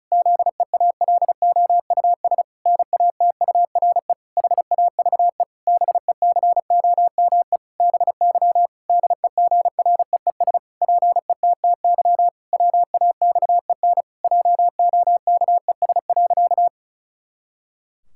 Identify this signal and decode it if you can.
Morse code, heard as a keyed tone, about 36 wpm, 700 Hz. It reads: ZEALOUS NATURE HAVE BECOME BY DEGREES PETTY WAXEN JOKES.
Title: Pangram Challenge v2 - 35wpm